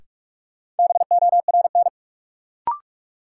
Morse code, heard as a keyed tone, about 48 wpm, 700 Hz. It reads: BORN
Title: Sentences from Top 1000 Words - New Word Drill - 45wpm